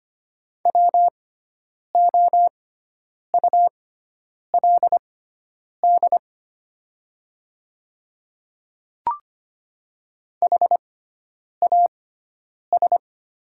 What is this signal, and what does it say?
Morse code, heard as a keyed tone, about 25 wpm, spaced as for 9 wpm, 700 Hz. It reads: WOULD HAS
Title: 6x Character Spacing - Top 100 Words - 25wpm